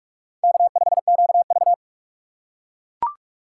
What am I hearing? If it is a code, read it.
Morse code, heard as a keyed tone, about 45 wpm, 700 Hz. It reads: K5QV
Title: USA Call Signs - 4 Characters - 4x Character Spacing - 45wpm